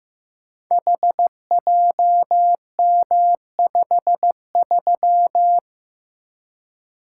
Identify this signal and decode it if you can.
Morse code, heard as a keyed tone, about 15 wpm, 700 Hz. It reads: HJM53
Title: Code Groups - 5 Letter:Numbers - 15wpm